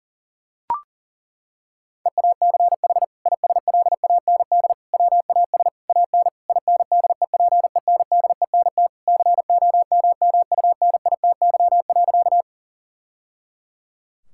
Morse code, heard as a keyed tone, about 40 wpm, 700 Hz. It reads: EACH ISLAND WAS AN INDEPENDENT COMMUNITY.
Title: Sentences from Top 1000 Words - Review Part 6 - 40wpm